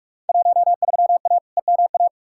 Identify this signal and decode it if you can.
Morse code, heard as a keyed tone, about 46 wpm, 700 Hz. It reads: 13A EMA
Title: Field Day Exchanges Encore - 45wpm